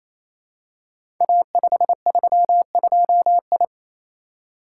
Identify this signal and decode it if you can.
Morse code, heard as a keyed tone, about 28 wpm, 700 Hz. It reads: A532I
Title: Code Groups - 5 Letter:Numbers - 28wpm